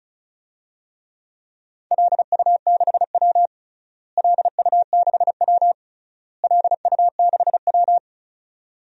Morse code, heard as a keyed tone, about 36 wpm, 700 Hz. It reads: LU6W LU6W LU6W
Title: DXCC-100 Call Signs - 4 Characters - Repeated 3x - 35wpm